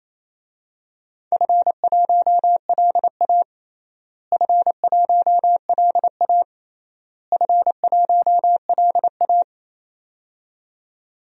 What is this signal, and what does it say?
Morse code, heard as a keyed tone, about 28 wpm, 700 Hz. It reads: F1LA F1LA F1LA
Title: DXCC-100 Call Signs - 4 Characters - Repeated 3x - 28wpm